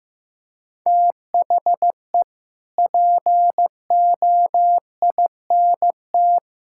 Morse code, heard as a keyed tone, about 15 wpm, 700 Hz. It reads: THE POINT
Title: Sentences from Top 1600 Words 15wpm